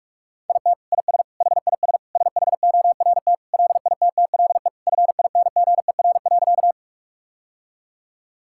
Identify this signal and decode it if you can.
Morse code, heard as a keyed tone, about 46 wpm, 700 Hz. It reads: IT IS HIS SHORT LITTLE FINGER.
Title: Sentences from Top 1600 Words 45wpm